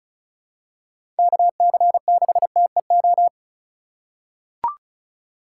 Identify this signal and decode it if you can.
Morse code, heard as a keyed tone, about 36 wpm, 690 Hz. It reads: KC6TEO
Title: USA Call Signs - 6 Characters - 3x Character Spacing - 35wpm